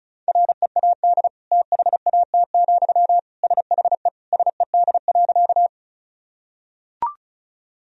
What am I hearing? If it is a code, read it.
Morse code, heard as a keyed tone, about 34 wpm, 700 Hz. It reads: READ THAT, SHE SED.
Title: Sentences from Top 200 Words and Abbreviations 35wpm